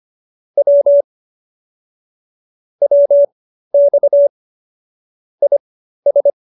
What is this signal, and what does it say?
Morse code, heard as a keyed tone, about 25 wpm, spaced as for 13 wpm, 570 Hz. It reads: W WX IS